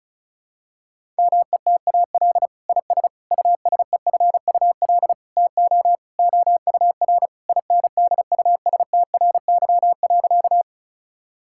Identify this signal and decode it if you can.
Morse code, heard as a keyed tone, about 34 wpm, 700 Hz. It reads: METAL IS USEFUL TO OUR INDUSTRY.
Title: Sentences from Top 1000 Words - Review Part 6 - 35wpm